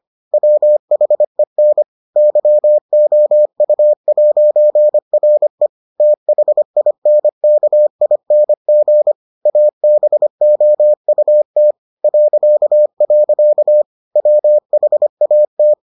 Morse code, heard as a keyed tone, about 25 wpm, 600 Hz. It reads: WHEN YOU'RE THINKING ABOUT .. WHAT